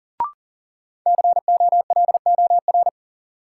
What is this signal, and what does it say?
Morse code, heard as a keyed tone, about 40 wpm, 700 Hz. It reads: COLOR
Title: Sentences from Top 1600 Words - New Word Drill Alt - 40wpm